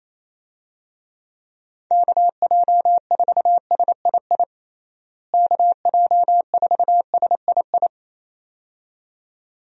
Morse code, heard as a keyed tone, about 28 wpm, 700 Hz. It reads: KJ4SII KJ4SII
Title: USA Call Signs - 6 Characters - Repeated 2x - 28wpm